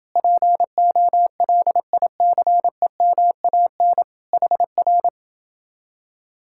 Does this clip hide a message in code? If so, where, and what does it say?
Yes: Morse code from 0.2 to 5.1 s, heard as a keyed tone, about 28 wpm, 700 Hz. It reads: POLICEMAN HR